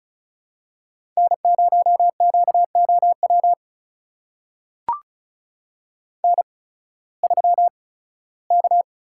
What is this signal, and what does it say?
Morse code, heard as a keyed tone, about 34 wpm, spaced as for 18 wpm, 700 Hz. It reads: N0QOW N3K
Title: USA Call Signs - 5 Characters - 8x Character Spacing - 35wpm